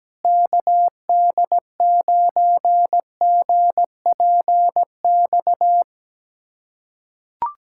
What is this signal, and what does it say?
Morse code, heard as a keyed tone, about 17 wpm, 700 Hz. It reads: KD9GPX
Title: USA Callsign DE Callsign 17wpm